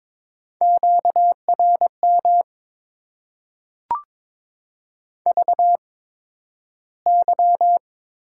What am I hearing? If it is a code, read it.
Morse code, heard as a keyed tone, about 22 wpm, spaced as for 12 wpm, 700 Hz. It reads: QRM VY
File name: Top 90 QSO Elements - 8x Character Spacing - 22wpm